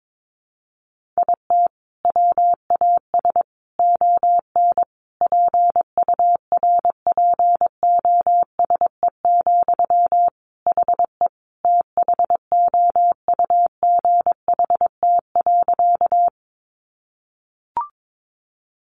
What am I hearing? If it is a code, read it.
Morse code, heard as a keyed tone, about 22 wpm, 700 Hz. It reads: IT WAS ON PURPOSE, HE THOUGHT.
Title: Sentences from Top 800 Words 22wpm